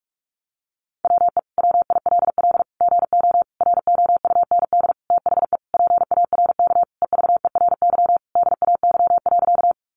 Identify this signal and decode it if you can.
Morse code, heard as a keyed tone, about 44 wpm, 700 Hz. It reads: WE WILL GO ROUND THE PARK EVERY DAY.
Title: Sentences from Top 1300 Words 45wpm